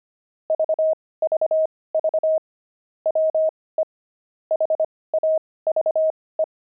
Morse code, heard as a keyed tone, about 25 wpm, spaced as for 18 wpm, 630 Hz. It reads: VVV WE HAVE
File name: pangram-fs-18-cs-25-nine